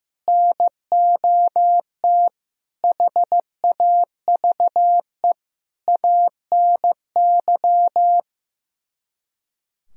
Morse code, heard as a keyed tone, about 15 wpm, 700 Hz. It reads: NOT HAVE ANY
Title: Speed-Racing - Sets of 5 Words 15wpm